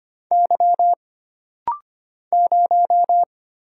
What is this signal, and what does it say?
Morse code, heard as a keyed tone, about 25 wpm, 700 Hz. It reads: Y 0
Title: Single Letter-Number - Rapid-Fire 25wpm